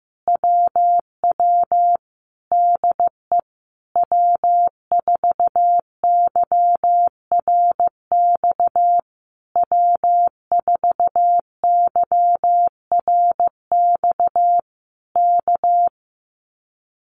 Morse code, heard as a keyed tone, about 15 wpm, 700 Hz. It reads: WW DE W4YRX W4YRX K
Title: Common QSO Phrases Encore 15wpm